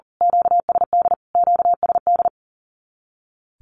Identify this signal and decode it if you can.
Morse code, heard as a keyed tone, about 40 wpm, 700 Hz. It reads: QSD QSD
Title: Q-Codes Repeated 3x 40wpm